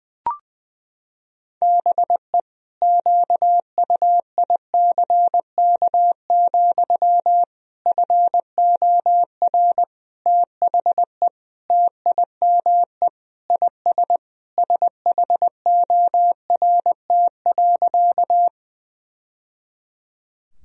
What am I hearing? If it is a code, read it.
Morse code, heard as a keyed tone, about 21 wpm, 700 Hz. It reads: BE QUICK, FOR THE TIME IS SHORT.
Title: Sentences from Top 1000 Words - Review Part 2 - 20wpm